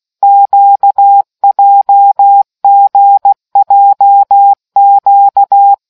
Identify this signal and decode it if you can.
Morse code, heard as a keyed tone, about 16 wpm, 800 Hz. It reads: QJGJQ